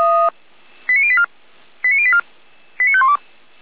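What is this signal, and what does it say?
Five-tone selective call (ZVEI), heard as quick runs of five tones, tones of about 70 ms. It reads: 89084 89084 89512
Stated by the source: ZVEI five-tone selective call received on SDR.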